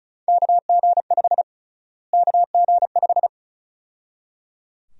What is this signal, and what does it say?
Morse code, heard as a keyed tone, about 34 wpm, 700 Hz. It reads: KG5 KG5
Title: USA Call Sign Prefixes - Repeated 2x - 35wpm